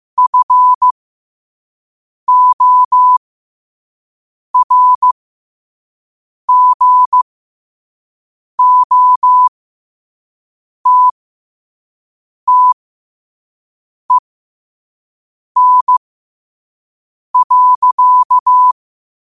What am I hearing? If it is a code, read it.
Morse code, heard as a keyed tone, about 15 wpm, spaced as for 5 wpm, 1000 Hz. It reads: FORGOTTEN.
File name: qotdfiles / 5wpm